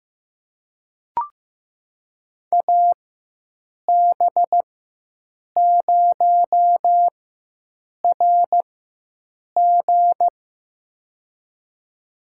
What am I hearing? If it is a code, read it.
Morse code, heard as a keyed tone, about 15 wpm, spaced as for 7 wpm, 700 Hz. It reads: AB0RG
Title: USA Call Signs - 5 Characters - 4x Character Spacing - 15wpm